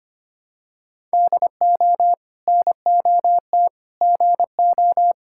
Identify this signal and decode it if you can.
Morse code, heard as a keyed tone, about 25 wpm, 700 Hz. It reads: DO NOT GO
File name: practice / 25wpm / Sentences with 2-Word Phrases - 25wpm